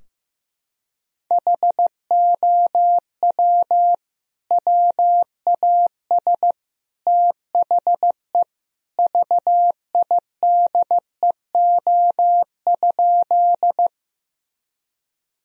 Morse code, heard as a keyed tone, about 16 wpm, 700 Hz. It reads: HOW WAS THE VIDEO?
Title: Sentences from Top 1000 Words - Review Part 4 - 15wpm